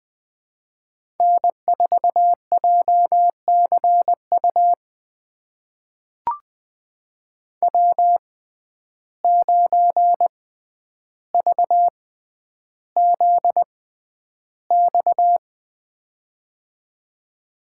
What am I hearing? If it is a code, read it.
Morse code, heard as a keyed tone, about 20 wpm, spaced as for 10 wpm, 700 Hz. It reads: N4JCU W9VZX
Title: USA Call Signs - 5 Characters - 6x Character Spacing - 20wpm